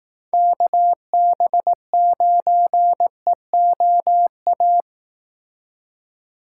Morse code, heard as a keyed tone, about 18 wpm, 700 Hz. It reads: KB9EOA
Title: Speed Racing - USA 2x3 Call Signs 15wpm